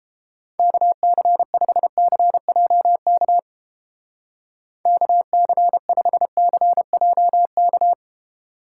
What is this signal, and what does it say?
Morse code, heard as a keyed tone, about 29 wpm, 700 Hz. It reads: KC5CJK KC5CJK